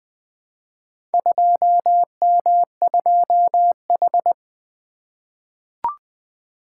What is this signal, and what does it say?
Morse code, heard as a keyed tone, about 20 wpm, 700 Hz. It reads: 2M2H
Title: Four Letter-Number Combinations - 20wpm